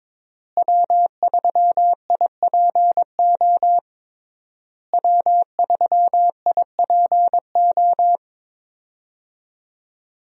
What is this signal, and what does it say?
Morse code, heard as a keyed tone, about 22 wpm, 700 Hz. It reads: W3IPO W3IPO
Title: USA Call Signs - 5 Characters - Repeated 3x - 22wpm